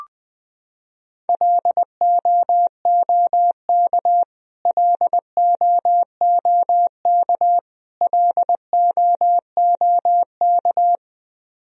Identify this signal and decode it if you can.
Morse code, heard as a keyed tone, about 20 wpm, 700 Hz. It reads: LOOK LOOK LOOK
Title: Top 100 Words Repeated 20wpm